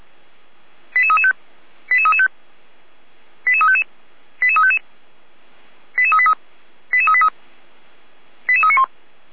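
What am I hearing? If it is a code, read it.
Five-tone selective call (ZVEI), heard as quick runs of five tones, tones of about 80 ms. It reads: 88335 88335 88355 88355 88383 88383 88391
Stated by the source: Five-tone ZVEI selective call received on an SDR.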